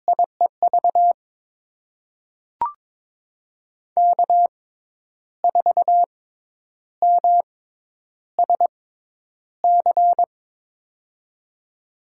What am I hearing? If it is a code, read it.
Morse code, heard as a keyed tone, about 23 wpm, spaced as for 10 wpm, 700 Hz. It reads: IEV K4MSC
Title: USA Call Signs - 5 Characters - 6x Character Spacing - 22wpm